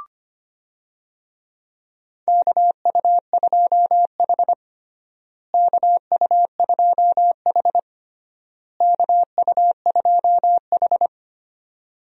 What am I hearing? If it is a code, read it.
Morse code, heard as a keyed tone, about 25 wpm, 700 Hz. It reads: KU2H KU2H KU2H
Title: USA Call Signs - 4 Characters - Repeated 3x - 25wpm